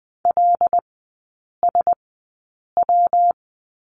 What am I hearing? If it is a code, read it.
Morse code, heard as a keyed tone, about 20 wpm, spaced as for 8 wpm, 700 Hz. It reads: LSW
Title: 138 Lesson 23e - Letter f - Three Characters 20wpm